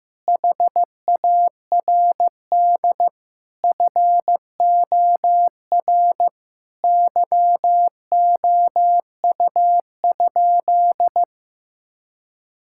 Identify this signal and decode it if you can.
Morse code, heard as a keyed tone, about 16 wpm, 700 Hz. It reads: HARD FOR YOU?
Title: Sentences from Top 1000 Words - Review Part 6 - 15wpm